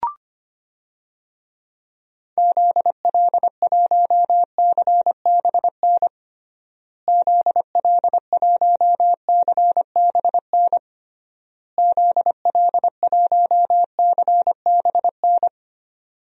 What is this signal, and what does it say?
Morse code, heard as a keyed tone, about 25 wpm, 700 Hz. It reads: ZL1CBN ZL1CBN ZL1CBN
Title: DXCC-100 Call Signs - 6 Characters - Repeated 3x - 25wpm